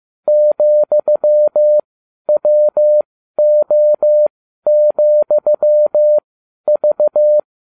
Morse code, heard as a keyed tone, about 15 wpm, spaced as for 11 wpm, 600 Hz. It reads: ,WO,V
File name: cw / ubacw20 3